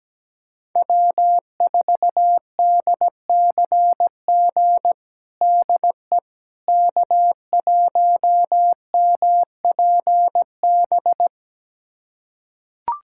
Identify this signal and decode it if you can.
Morse code, heard as a keyed tone, about 17 wpm, 700 Hz. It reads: W4DCG DE K1MPB
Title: USA Callsign DE Callsign - 5 Characters - 17wpm